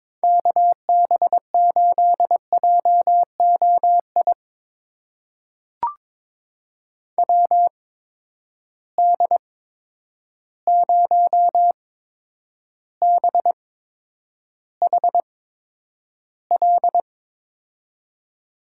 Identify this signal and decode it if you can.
Morse code, heard as a keyed tone, about 22 wpm, spaced as for 9 wpm, 700 Hz. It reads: KB8JOI WD0BHL
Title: USA Call Signs - 6 Characters - 8x Character Spacing - 22wpm